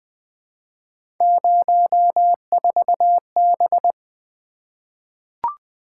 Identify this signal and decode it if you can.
Morse code, heard as a keyed tone, about 20 wpm, 700 Hz. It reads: 04B
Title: Three Letter-Number Combinations - 20wpm